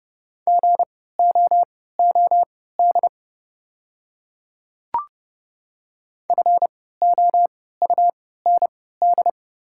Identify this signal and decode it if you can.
Morse code, heard as a keyed tone, about 30 wpm, spaced as for 17 wpm, 690 Hz. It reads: GOOD FOUND
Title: 3x Character Spacing - Top 300 Words - 30wpm